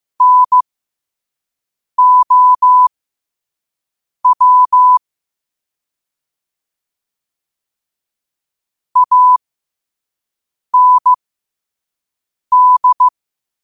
Morse code, heard as a keyed tone, about 15 wpm, spaced as for 5 wpm, 1000 Hz. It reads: NOW AND
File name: qotdfiles / 5wpm